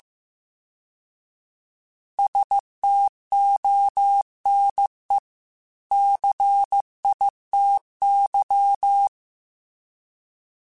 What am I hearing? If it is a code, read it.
Morse code, heard as a keyed tone, about 15 wpm, 800 Hz. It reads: STONE CITY